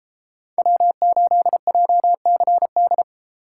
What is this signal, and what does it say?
Morse code, heard as a keyed tone, about 32 wpm, 700 Hz. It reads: W8JCD